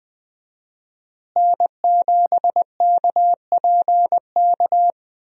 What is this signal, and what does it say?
Morse code, heard as a keyed tone, about 20 wpm, 700 Hz. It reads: N7KPK
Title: USA Call Signs - 5 Characters - Repeated 2x - 20wpm